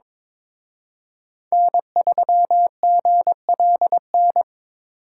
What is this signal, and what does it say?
Morse code, heard as a keyed tone, about 26 wpm, 700 Hz. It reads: N3GLN
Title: Speed Racing - USA 1x3 Call Signs 22wpm